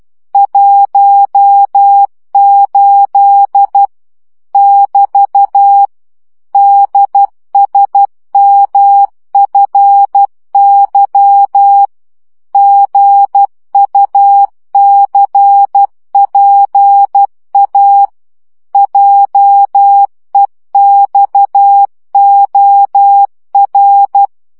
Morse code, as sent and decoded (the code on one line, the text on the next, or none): .---- ---.. / -...- / -.. ... -- ..-. -.-- / --. ..- -.-. .--. .- / .--- . -..- --- .-.
18 <BT> DSMFY GUCPA JEXOR